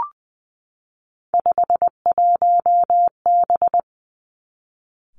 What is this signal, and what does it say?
Morse code, heard as a keyed tone, about 20 wpm, 700 Hz. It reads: 51B
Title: Three Challenging Letter-Number Combinations 20wpm